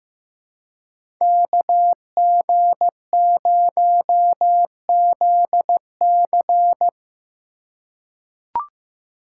Morse code, heard as a keyed tone, about 15 wpm, 700 Hz. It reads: KG0ZC
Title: USA Call Signs - 4x Character Spacing - 15wpm